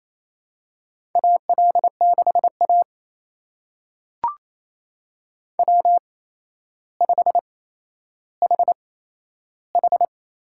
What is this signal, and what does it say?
Morse code, heard as a keyed tone, about 28 wpm, spaced as for 12 wpm, 700 Hz. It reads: AL6A W5HH
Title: USA Call Signs - 4 Characters - 8x Character Spacing - 28wpm